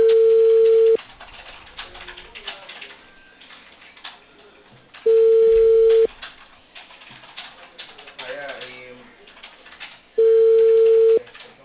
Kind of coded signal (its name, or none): telephone call progress tone